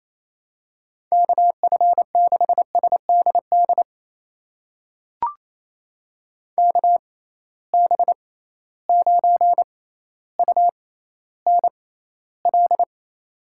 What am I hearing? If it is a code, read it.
Morse code, heard as a keyed tone, about 28 wpm, spaced as for 14 wpm, 700 Hz. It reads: KF6SDD KB9UNL
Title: USA Call Signs - 6 Characters - 6x Character Spacing - 28wpm